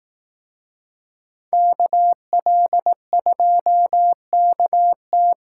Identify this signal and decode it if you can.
Morse code, heard as a keyed tone, about 18 wpm, 700 Hz. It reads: KL2KT
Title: Speed Racing - USA Call Signs 15wpm